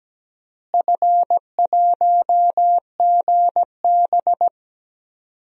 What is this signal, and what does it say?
Morse code, heard as a keyed tone, about 17 wpm, 700 Hz. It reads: F1GB
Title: DXCC-100 Call Signs - 4 Characters - Repeated 2x - 17wpm